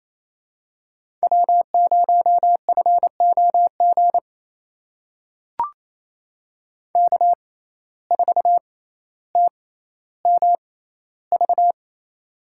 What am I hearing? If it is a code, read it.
Morse code, heard as a keyed tone, about 28 wpm, spaced as for 14 wpm, 700 Hz. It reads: W0FOG K4TMV
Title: USA Call Signs - 5 Characters - 6x Character Spacing - 28wpm